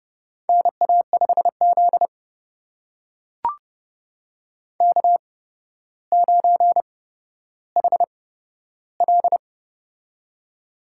Morse code, heard as a keyed tone, about 30 wpm, spaced as for 13 wpm, 700 Hz. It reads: NA5Z K9HL
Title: USA Call Signs - 4 Characters - 8x Character Spacing - 30wpm